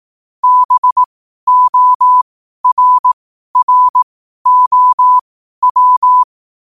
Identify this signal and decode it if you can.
Morse code, heard as a keyed tone, about 18 wpm, spaced as for 12 wpm, 1000 Hz. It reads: BORROW
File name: 12wpm